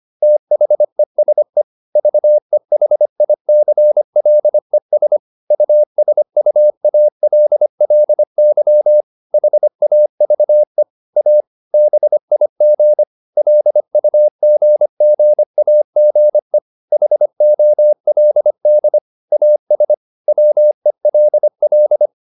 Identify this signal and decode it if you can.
Morse code, heard as a keyed tone, about 25 wpm, 600 Hz. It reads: THESE VEHICLES USUALLY HAVE A BIG LUGGAGE HOLD AS WELL
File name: Never 0025wpm